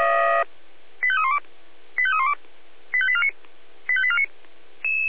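Five-tone selective call (ZVEI), heard as quick runs of five tones, tones of about 70 ms. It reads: 85312 85312 85849 85849
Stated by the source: ZVEI five-tone selective call received with an SDR.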